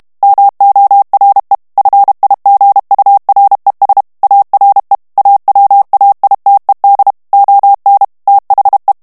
Morse code, heard as a keyed tone, about 32 wpm, 800 Hz. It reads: MORE FIGURES ARE AWAITED ON THE
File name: ANGLAIS texte clair 2 vitesse 30 mots minute